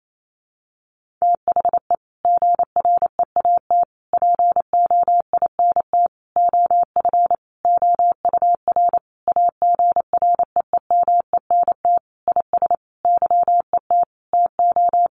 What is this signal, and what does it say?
Morse code, heard as a keyed tone, about 28 wpm, 700 Hz. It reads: THE GREAT POINT OF OUR AGREEMENT IS YET TO
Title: Sentences from Top 1500 Words 28wpm